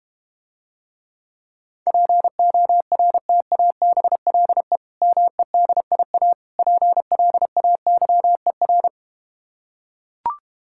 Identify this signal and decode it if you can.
Morse code, heard as a keyed tone, about 32 wpm, 700 Hz. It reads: PORTABLE MEDIA PLAYER